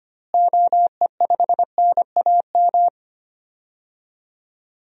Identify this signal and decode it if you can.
Morse code, heard as a keyed tone, about 25 wpm, 700 Hz. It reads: OE5NAM
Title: DXCC-100 Call Signs - 6 Characters - Repeated 2x - 25wpm